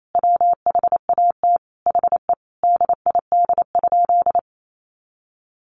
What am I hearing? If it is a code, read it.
Morse code, heard as a keyed tone, about 28 wpm, 700 Hz. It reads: WHAT HE DID?
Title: Sentences from Top 100 Words 28wpm